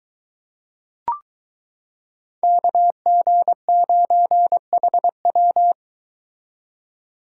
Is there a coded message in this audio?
Yes: Morse code from 2.4 to 5.7 s, heard as a keyed tone, about 22 wpm, 700 Hz. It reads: KG9HW